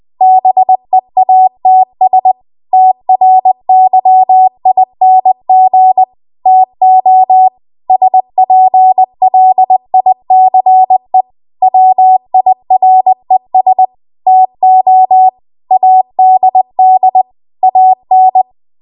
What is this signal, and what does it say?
Morse code, heard as a keyed tone, about 20 wpm, 750 Hz. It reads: BEATS TRYING TO SPLICE WIRES TO ADD AN
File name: Archive / 20 WPM / 20WPM